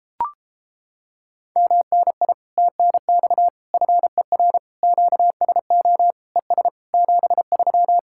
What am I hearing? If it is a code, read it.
Morse code, heard as a keyed tone, about 31 wpm, 700 Hz. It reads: MNI TNX FER QSO ES 73